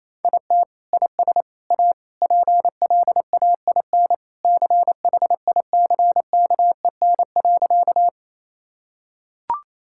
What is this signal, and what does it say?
Morse code, heard as a keyed tone, about 28 wpm, 700 Hz. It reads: IT IS A PLAIN CHICKEN.
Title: Sentences from Top 1000 Words - Review Part 5 - 28wpm